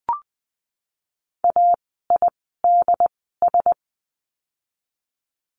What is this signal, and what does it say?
Morse code, heard as a keyed tone, about 20 wpm, spaced as for 14 wpm, 700 Hz. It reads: AIDS
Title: 068 Lesson 12f - Letter d - Words 20wpm